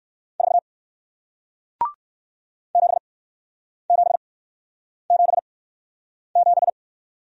Morse code, heard as a keyed tone, about 55 wpm, spaced as for 9 wpm, 700 Hz. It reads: 4 7777
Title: Speed-Racing Single Letter-Number 45wpm